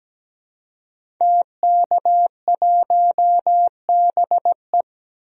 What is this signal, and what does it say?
Morse code, heard as a keyed tone, about 17 wpm, 700 Hz. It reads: TK1BE
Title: DXCC-100 Call Signs - 5 Characters - 17wpm